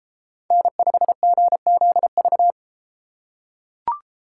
Morse code, heard as a keyed tone, about 33 wpm, 700 Hz. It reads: N5GZV